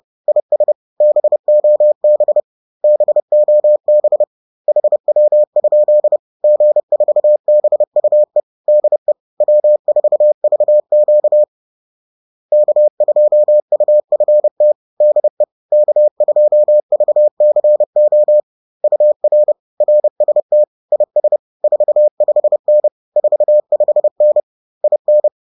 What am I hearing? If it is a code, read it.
Morse code, heard as a keyed tone, about 30 wpm, 600 Hz. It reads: IS BOB BOB HW? G4BUE DE W4VQ K2UFT DE K2VCO UR RST IS 45N 45N IN